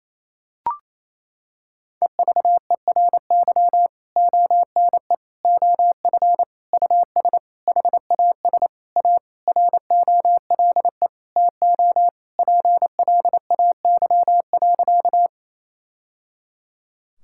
Morse code, heard as a keyed tone, about 28 wpm, 700 Hz. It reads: EVERY ONE OF US HAS A ROLE TO PLAY.
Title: Sentences from Top 1000 Words - Review Part 6 - 28wpm